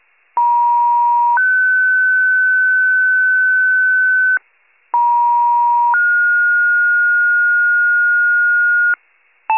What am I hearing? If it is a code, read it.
Two-tone page, heard as a one-second tone, then a three-second tone, about 950 then 1500 Hz, about 950 then 1430 Hz, with noise. It is heard from 0.4 to 8.9 s.
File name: wcalert